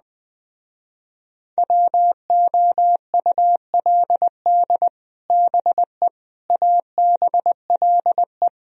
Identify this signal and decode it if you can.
Morse code, heard as a keyed tone, about 20 wpm, 700 Hz. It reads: WOULD BE ABLE